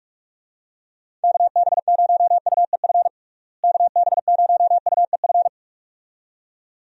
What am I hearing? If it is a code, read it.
Morse code, heard as a keyed tone, about 42 wpm, 700 Hz. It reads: KB0UEF KB0UEF
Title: USA Call Signs - 6 Characters - Repeated 2x - 45wpm